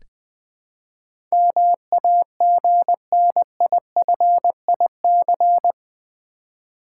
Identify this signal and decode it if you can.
Morse code, heard as a keyed tone, about 20 wpm, 700 Hz. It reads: MAGNIFIC
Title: Build A Word - 4 Letters At A Time - 11-Letter Words - 20wpm